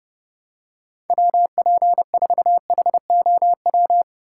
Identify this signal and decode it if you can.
Morse code, heard as a keyed tone, about 28 wpm, 700 Hz. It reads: WP4HOW